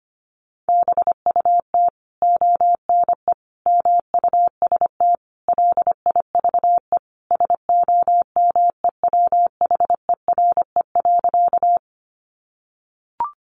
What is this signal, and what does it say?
Morse code, heard as a keyed tone, about 25 wpm, 700 Hz. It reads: BUT ONE MUST LIVE SOMEWHERE.
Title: Sentences from Top 1400 Words 25wpm